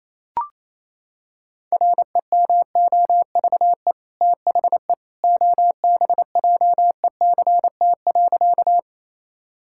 Morse code, heard as a keyed tone, about 28 wpm, 700 Hz. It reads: REMOVE THE OBJECT.
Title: Sentences from Top 1000 Words - Review Part 5 - 28wpm